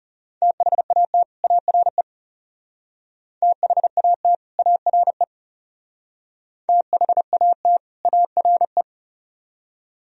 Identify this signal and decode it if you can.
Morse code, heard as a keyed tone, about 39 wpm, 700 Hz. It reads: THAT ARE THAT ARE THAT ARE